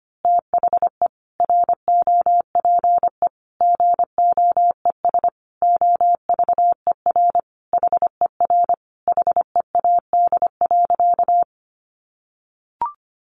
Morse code, heard as a keyed tone, about 25 wpm, 700 Hz. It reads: THE ROPE GOES OVER HER HEAD.
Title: Easy Phrases 25wpm